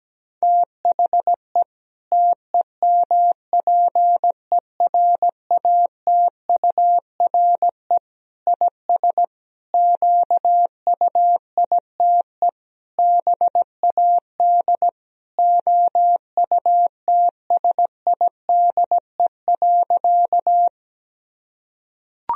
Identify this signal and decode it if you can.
Morse code, heard as a keyed tone, about 17 wpm, 700 Hz. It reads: THE TEMPERATURE IS QUITE BAD OUTSIDE.
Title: Sentences from Top 1000 Words - Review Part 4 - 17wpm